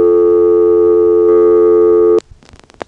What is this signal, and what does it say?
Dial tone (North America), 350 + 440 Hz.